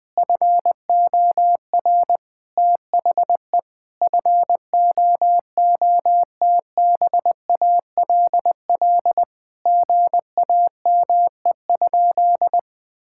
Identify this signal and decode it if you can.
Morse code, heard as a keyed tone, about 20 wpm, 700 Hz. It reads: FOR THE FOOTBALL GAME?
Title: Sentences from Top 1600 Words 20wpm